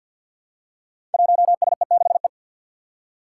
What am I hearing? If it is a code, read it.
Morse code, heard as a keyed tone, about 51 wpm, 700 Hz. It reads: 1SE6E
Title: Five Challenging Letter-Number Combinations 50wpm